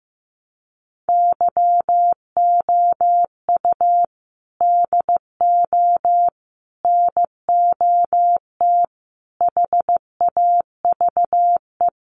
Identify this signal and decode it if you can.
Morse code, heard as a keyed tone, about 15 wpm, 700 Hz. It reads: YOU DO NOT HAVE
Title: Sets of 5 Words Encore 15wpm